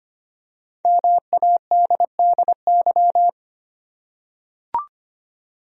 Morse code, heard as a keyed tone, about 25 wpm, 700 Hz. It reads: MADDY
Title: USA Common Names - 25wpm